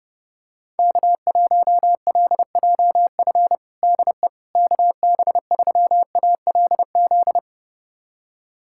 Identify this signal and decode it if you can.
Morse code, heard as a keyed tone, about 30 wpm, 700 Hz. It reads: K1LJF DE KB3ALZ
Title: USA Callsign DE Callsign 30wpm